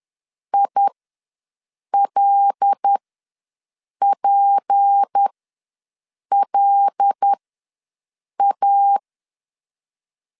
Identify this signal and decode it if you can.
Morse code, heard as a keyed tone, about 10 wpm, spaced as for 6 wpm, 800 Hz. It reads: ILPLA
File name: LEZIONI / 6 PKX0 / 6 P ws 10wpm 3751